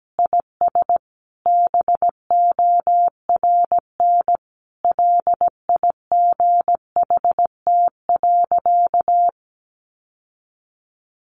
Morse code, heard as a keyed tone, about 17 wpm, 700 Hz. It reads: IS BORN LIGHT.
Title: Sentences from Top 1100 Words 17wpm